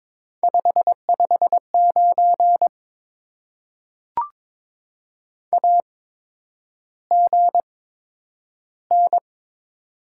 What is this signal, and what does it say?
Morse code, heard as a keyed tone, about 23 wpm, spaced as for 10 wpm, 700 Hz. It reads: 559 AGN
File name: Top 90 QSO Elements - 8x Character Spacing - 22wpm